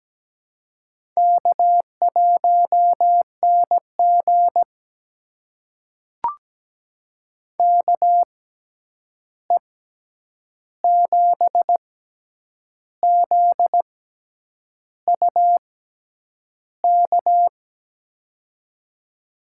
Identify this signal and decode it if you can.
Morse code, heard as a keyed tone, about 17 wpm, spaced as for 8 wpm, 700 Hz. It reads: K1NG KE7ZUK
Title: USA Call Signs - 6x Character Spacing - 17wpm